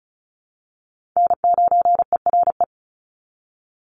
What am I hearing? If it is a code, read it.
Morse code, heard as a keyed tone, about 34 wpm, 700 Hz. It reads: N9ERE
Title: USA Call Signs 35wpm